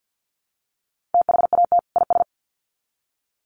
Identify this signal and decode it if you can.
Morse code, heard as a keyed tone, about 50 wpm, 700 Hz. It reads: THAT IS
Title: Sets of 2 Words 50wpm